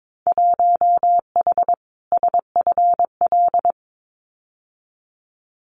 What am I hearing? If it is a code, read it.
Morse code, heard as a keyed tone, about 22 wpm, 700 Hz. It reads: 1H SFL
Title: Winter Field Day Exchanges 22wpm